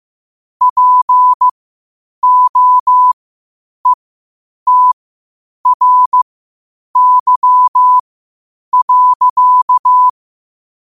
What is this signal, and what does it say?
Morse code, heard as a keyed tone, about 15 wpm, spaced as for 8 wpm, 1000 Hz. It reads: POETRY.
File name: qotdfiles / 8wpm